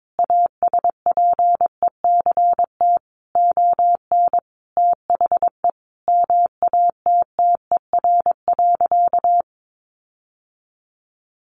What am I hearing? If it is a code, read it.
Morse code, heard as a keyed tone, about 22 wpm, 700 Hz. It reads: ASPECT ON THE MATTER.
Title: Sentences from Top 1300 Words 22wpm